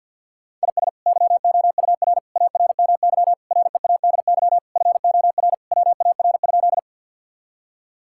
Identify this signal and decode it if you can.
Morse code, heard as a keyed tone, about 50 wpm, 700 Hz. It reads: IS YOUR ARMY READY FOR WAR?
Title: Sentences from Top 1000 Words - Review Part 5 - 50wpm